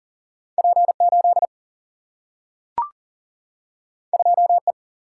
Morse code, heard as a keyed tone, about 40 wpm, 700 Hz. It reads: P 8 2 E
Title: Single-Letter Single-Number 40wpm